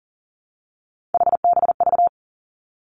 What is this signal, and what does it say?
Morse code, heard as a keyed tone, about 38 wpm, 700 Hz. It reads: HBV
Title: Three Challenging Letter-Number Combinations 40wpm